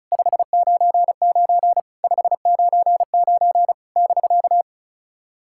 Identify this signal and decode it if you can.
Morse code, heard as a keyed tone, about 36 wpm, 690 Hz. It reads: 599 599 <BK>
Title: SOTA QSO Practice 35wpm